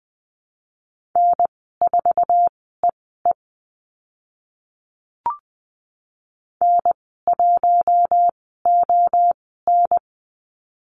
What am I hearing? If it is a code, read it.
Morse code, heard as a keyed tone, about 20 wpm, spaced as for 14 wpm, 700 Hz. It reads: N4EE N1ON
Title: 062 Lesson 11g - Letter h - Call Signs 20wpm